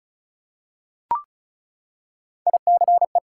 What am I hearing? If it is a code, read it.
Morse code, heard as a keyed tone, about 36 wpm, 700 Hz. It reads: ICE
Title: Three Letter Words - 35wpm